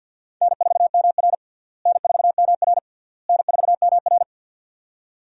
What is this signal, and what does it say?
Morse code, heard as a keyed tone, about 48 wpm, 700 Hz. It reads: N4MR N4MR N4MR
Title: USA Call Signs - 4 Characters - Repeated 3x - 50wpm